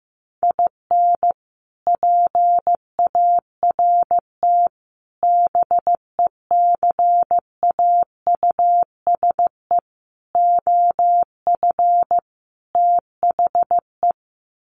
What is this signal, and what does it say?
Morse code, heard as a keyed tone, about 15 wpm, 700 Hz. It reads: IN PART BECAUSE OF THE
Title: Sets of 5 Words Encore2 15wpm